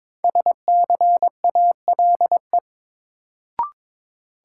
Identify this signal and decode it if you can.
Morse code, heard as a keyed tone, about 22 wpm, 700 Hz. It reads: SCALE
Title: Sentences from Top 1000 Words - Review Part 4 - New Word Drill - 22wpm